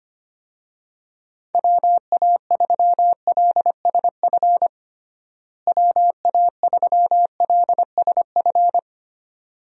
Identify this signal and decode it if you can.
Morse code, heard as a keyed tone, about 25 wpm, 700 Hz. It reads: WA3LSF WA3LSF
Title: USA Call Signs - 6 Characters - Repeated 2x - 25wpm